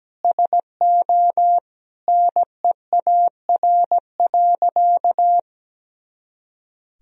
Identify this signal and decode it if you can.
Morse code, heard as a keyed tone, about 17 wpm, 700 Hz. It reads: SO NEAR.
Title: Sentences from Top 300 Words and Abbreviations 17wpm